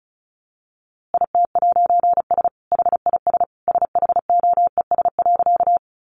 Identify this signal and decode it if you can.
Morse code, heard as a keyed tone, about 34 wpm, 700 Hz. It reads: IT'S HIS SHOES.
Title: A Sound Of Thunder 35wpm